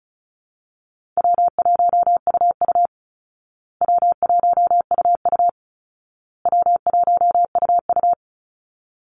Morse code, heard as a keyed tone, about 34 wpm, 700 Hz. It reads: W1UU W1UU W1UU
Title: Call Signs Repeated 2x - 35wpm